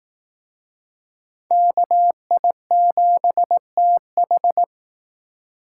Morse code, heard as a keyed tone, about 18 wpm, 700 Hz. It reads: KI7TH
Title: Speed Racing - USA Call Signs 15wpm